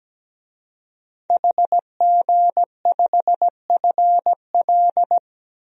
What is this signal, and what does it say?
Morse code, heard as a keyed tone, about 17 wpm, 700 Hz. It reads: HG5FL
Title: DXCC-100 Call Signs - 5 Characters - 17wpm